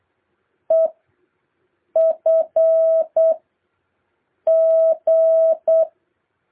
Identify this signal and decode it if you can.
Morse code, heard as a keyed tone, about 8 wpm, spaced as for 5 wpm, 650 Hz. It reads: EFG